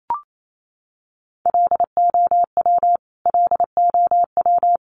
Morse code, heard as a keyed tone, about 28 wpm, 700 Hz. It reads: LOW LOW
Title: Top 200 Words Repeated 28wpm